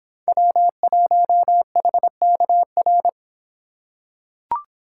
Morse code, heard as a keyed tone, about 26 wpm, 700 Hz. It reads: W1HKR